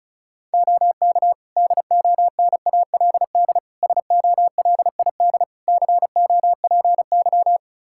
Morse code, heard as a keyed tone, about 36 wpm, 700 Hz. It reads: OK DONALD SOLID COPY
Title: Practice QSO - Exchange 3 - More Info - 35wpm